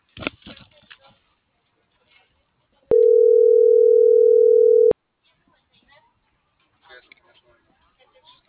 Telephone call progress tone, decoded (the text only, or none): ringback tone